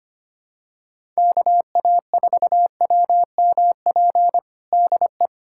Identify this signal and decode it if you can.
Morse code, heard as a keyed tone, about 25 wpm, 700 Hz. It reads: KA4WMP DE
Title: USA Callsign DE Callsign - 6 Characters - 25wpm